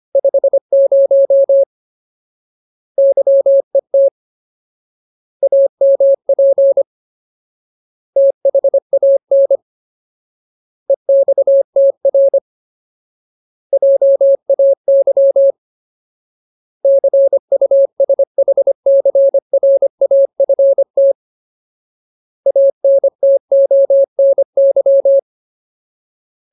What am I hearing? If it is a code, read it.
Morse code, heard as a keyed tone, about 25 wpm, 550 Hz. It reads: 50 YET AMP THAN EXTR JAY CUSHCRAFT ANTONY
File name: Ultimate Word List / 25WPM Ultimate